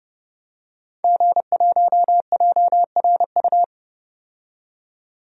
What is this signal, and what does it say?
Morse code, heard as a keyed tone, about 30 wpm, 700 Hz. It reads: G1JRU
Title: DXCC-100 Call Signs - 5 Characters - 30wpm